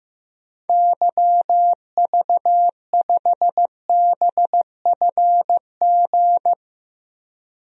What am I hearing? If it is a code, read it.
Morse code, heard as a keyed tone, about 15 wpm, 700 Hz. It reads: YV5BFG
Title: DXCC-100 Call Signs - 6 Characters - 15wpm